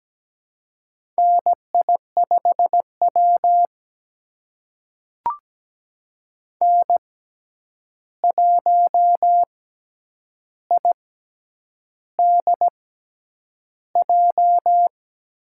Morse code, heard as a keyed tone, about 17 wpm, spaced as for 8 wpm, 700 Hz. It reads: NI5W N1IDJ
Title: USA Call Signs - 6x Character Spacing - 17wpm